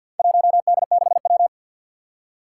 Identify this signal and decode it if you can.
Morse code, heard as a keyed tone, about 52 wpm, 700 Hz. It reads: 1D6W
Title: Four Challenging Letter-Number Combinations 50wpm